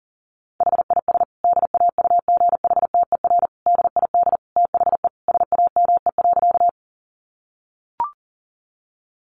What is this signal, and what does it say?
Morse code, heard as a keyed tone, about 37 wpm, 700 Hz. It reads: HIS DAUGHTER DID THE SAME.
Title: Sentences from Top 1500 Words 40wpm